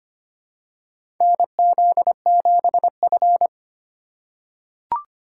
Morse code, heard as a keyed tone, about 25 wpm, 700 Hz. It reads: NZ7F
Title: USA Call Signs - 4 Characters - 4x Character Spacing - 25wpm